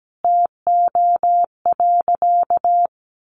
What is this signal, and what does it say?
Morse code, heard as a keyed tone, about 17 wpm, 700 Hz. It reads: TO.
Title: A Sound Of Thunder 17wpm